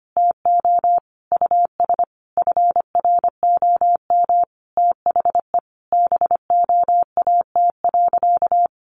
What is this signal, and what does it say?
Morse code, heard as a keyed tone, about 25 wpm, 700 Hz. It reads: TO US FROM THE BOAT.
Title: Sentences from Top 1400 Words 25wpm